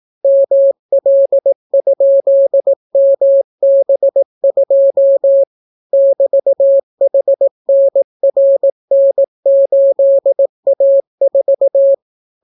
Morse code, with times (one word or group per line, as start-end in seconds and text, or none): 0.2-5.4 ML?MB2
5.9-11.9 <BT>HNRN8A4